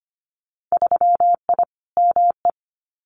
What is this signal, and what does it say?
Morse code, heard as a keyed tone, about 25 wpm, 690 Hz. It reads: 3I ME
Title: Winter Field Day Exchanges 25wpm